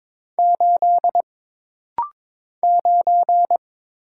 Morse code, heard as a keyed tone, about 22 wpm, 700 Hz. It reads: 8 9
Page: This rapid-fire timing is geared to push your ICR (Instant Character Recognition) skills.